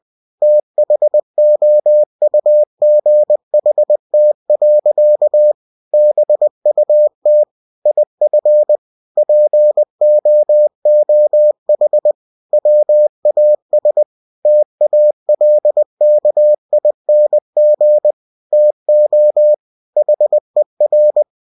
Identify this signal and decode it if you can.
Morse code, heard as a keyed tone, about 20 wpm, 600 Hz. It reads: THOUGHT. BUT IF POOH WAS TALKING TO HER
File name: cw-books / winnie-the-pooh / winnie-20wpm-page-0064